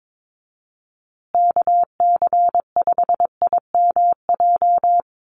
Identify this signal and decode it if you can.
Morse code, heard as a keyed tone, about 22 wpm, 700 Hz. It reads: KC5IMJ
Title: USA Call Signs Repeated 3x 22wpm